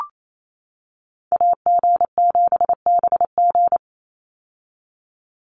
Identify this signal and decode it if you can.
Morse code, heard as a keyed tone, about 28 wpm, 700 Hz. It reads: AG7BG
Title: USA Call Signs 28wpm